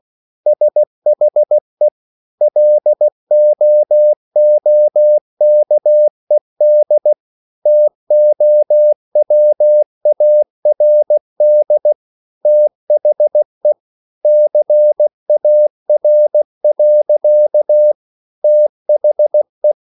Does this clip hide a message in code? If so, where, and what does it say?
Yes: Morse code from 0.5 to 19.7 s, heard as a keyed tone, about 16 wpm, 600 Hz. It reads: SHE LOOKED TOWARD THE CAR. THE